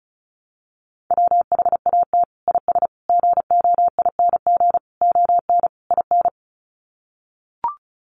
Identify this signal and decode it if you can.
Morse code, heard as a keyed tone, about 34 wpm, 700 Hz. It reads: WHAT IS GOING ON IN
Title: Sets of 5 Words Encore2 35wpm